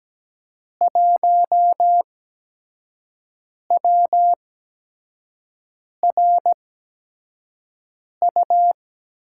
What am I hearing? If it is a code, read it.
Morse code, heard as a keyed tone, about 17 wpm, spaced as for 5 wpm, 700 Hz. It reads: 1WRU
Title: USA Call Signs - 6 Characters - 8x Character Spacing - 17wpm